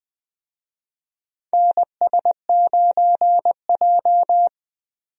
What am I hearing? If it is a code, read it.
Morse code, heard as a keyed tone, about 20 wpm, 700 Hz. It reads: NS9J
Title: USA Call Signs - 4 Characters - 20wpm